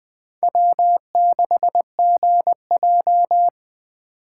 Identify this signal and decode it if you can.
Morse code, heard as a keyed tone, about 20 wpm, 700 Hz. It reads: W6GJ
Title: Speed Racing - USA 1x2 Call Signs 17wpm